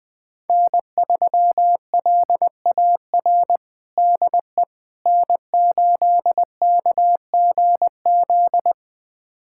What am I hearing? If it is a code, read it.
Morse code, heard as a keyed tone, about 21 wpm, 700 Hz. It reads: N3LAR DE N8KGZ
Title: USA Callsign DE Callsign - 5 Characters - 20wpm